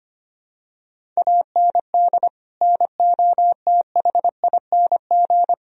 Morse code, heard as a keyed tone, about 25 wpm, 700 Hz. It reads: AND NOTHING
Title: Sentences from Top 1000 Words - Review Part 1 - 25wpm